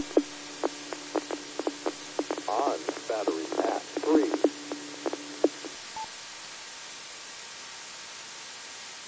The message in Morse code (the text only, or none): none